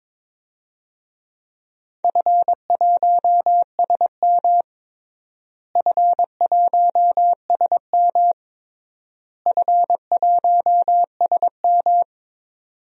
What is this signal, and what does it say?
Morse code, heard as a keyed tone, about 22 wpm, 700 Hz. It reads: F1SM F1SM F1SM
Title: DXCC-100 Call Signs - 4 Characters - Repeated 3x - 22wpm